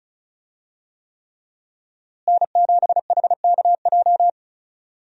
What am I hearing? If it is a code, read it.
Morse code, heard as a keyed tone, about 36 wpm, 700 Hz. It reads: N7HKJ
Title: USA Call Signs - 5 Characters - Repeated 3x - 35wpm